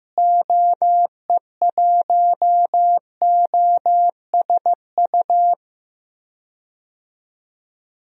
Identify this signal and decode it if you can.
Morse code, heard as a keyed tone, about 15 wpm, 700 Hz. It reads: OE1OSU
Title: DXCC-100 Call Signs - 6 Characters - Repeated 2x - 15wpm